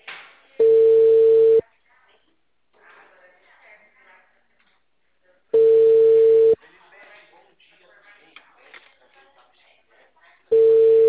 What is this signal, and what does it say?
Ringback tone (China, 450 Hz), with speech.
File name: out- -2009- - - .18